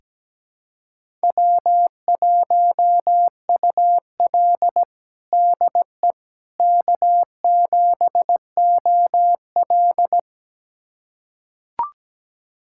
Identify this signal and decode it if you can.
Morse code, heard as a keyed tone, about 17 wpm, 700 Hz. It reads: W1UL DE K7OL
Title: USA Callsign DE Callsign - 4 Characters - 17wpm